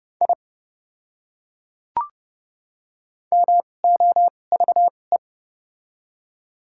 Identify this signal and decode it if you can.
Morse code, heard as a keyed tone, about 30 wpm, spaced as for 8 wpm, 700 Hz. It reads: IMOVE
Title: 2x Character Spacing - Top 300 Words - 30wpm